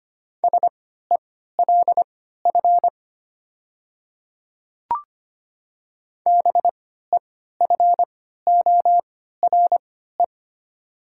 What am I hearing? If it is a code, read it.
Morse code, heard as a keyed tone, about 25 wpm, spaced as for 14 wpm, 700 Hz. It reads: SELF BEFORE
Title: 3x Character Spacing - Top 300 Words - 25wpm